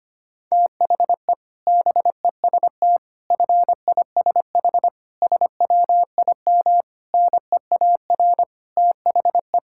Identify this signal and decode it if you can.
Morse code, heard as a keyed tone, about 25 wpm, 700 Hz. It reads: THE BEST FISH SWIM NEAR THE
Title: Sentences from Top 1000 Words - Review Part 6 - 25wpm